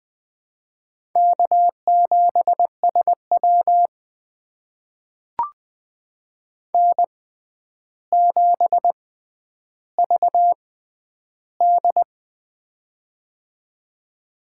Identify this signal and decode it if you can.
Morse code, heard as a keyed tone, about 20 wpm, spaced as for 10 wpm, 700 Hz. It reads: K7SW N7VD
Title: USA Call Signs - 4 Characters - 6x Character Spacing - 20wpm